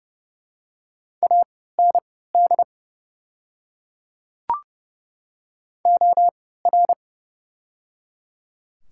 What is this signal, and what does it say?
Morse code, heard as a keyed tone, about 30 wpm, spaced as for 17 wpm, 700 Hz. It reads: AND OR
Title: 3x Character Spacing - Top 100 Words - 30wpm